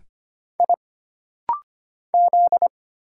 Morse code, heard as a keyed tone, about 25 wpm, 700 Hz. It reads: I Z
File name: Single Character (Letter:Number:Punctuation:Prosign) - Rapid-Fire - 25wpm